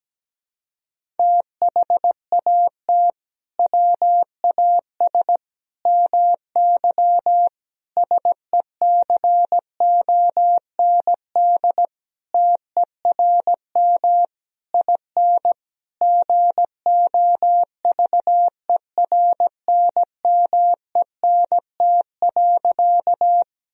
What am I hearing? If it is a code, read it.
Morse code, heard as a keyed tone, about 17 wpm, 700 Hz. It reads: THAT WAS MY SECOND TERM IN GOVERNMENT.
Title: Sentences from Top 1000 Words - Review Part 4 - 17wpm